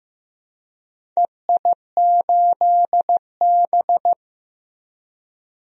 Morse code, heard as a keyed tone, about 16 wpm, 700 Hz. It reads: EI8B
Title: DXCC-100 Call Signs - 4 Characters - Repeated 3x - 15wpm